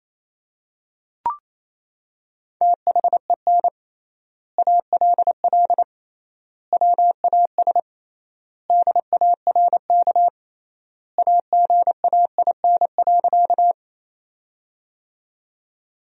Morse code, heard as a keyed tone, about 28 wpm, 700 Hz. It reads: THEN ALL WAS DARK AGAIN.
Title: 3x Word-Spacing - Sentences from Top 500 Words - 28wpm